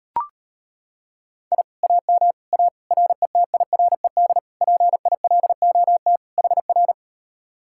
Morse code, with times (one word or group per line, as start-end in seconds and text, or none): 1.5-1.6 I
1.8-2.3 AM
2.5-2.7 A
2.9-4.4 RETIRED
4.6-6.2 PILOT
6.4-6.9 HR